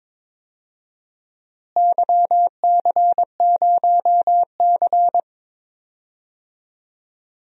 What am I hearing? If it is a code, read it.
Morse code, heard as a keyed tone, about 22 wpm, 700 Hz. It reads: YC0C
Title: DXCC-100 Call Signs - 4 Characters - Repeated 2x - 22wpm